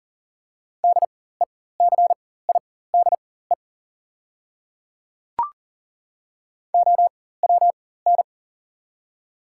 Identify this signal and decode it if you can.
Morse code, heard as a keyed tone, about 40 wpm, spaced as for 19 wpm, 700 Hz. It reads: DECIDE OWN
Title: 4x Character Spacing - Top 500 Words - 40wpm